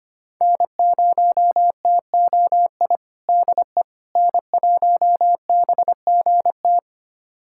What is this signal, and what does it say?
Morse code, heard as a keyed tone, about 25 wpm, 700 Hz. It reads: N0TOI DE N1BGT
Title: USA Callsign DE Callsign - 5 Characters - 25wpm